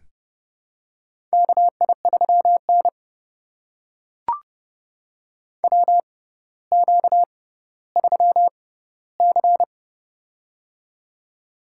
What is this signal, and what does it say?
Morse code, heard as a keyed tone, about 30 wpm, spaced as for 15 wpm, 700 Hz. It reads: KI3N WQ3C
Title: USA Call Signs - 4 Characters - 6x Character Spacing - 30wpm